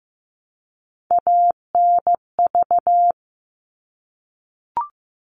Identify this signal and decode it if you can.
Morse code, heard as a keyed tone, about 15 wpm, 700 Hz. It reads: ANV
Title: All Cut Numbers - 3-Digit 15wpm